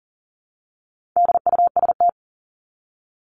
Morse code, heard as a keyed tone, about 40 wpm, 700 Hz. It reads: DUST
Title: Top 1751-2000 Words 40wpm